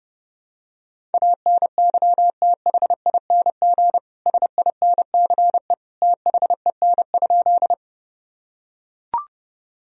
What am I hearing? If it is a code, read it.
Morse code, heard as a keyed tone, about 32 wpm, 700 Hz. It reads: ANYTHING SINCE THEN?
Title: Sentences from Top 1000 Words - Review Part 1 - 30wpm